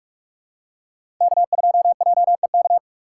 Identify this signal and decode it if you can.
Morse code, heard as a keyed tone, about 46 wpm, 700 Hz. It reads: K2JEK
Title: USA Call Signs - 5 Characters - 2x Character Spacing - 45wpm